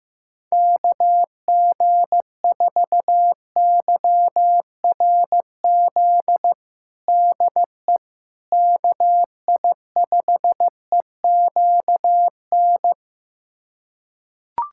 Morse code, heard as a keyed tone, about 15 wpm, 700 Hz. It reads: KG4YRZ DE KI5EQN
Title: USA Callsign DE Callsign - 6 Characters - 15wpm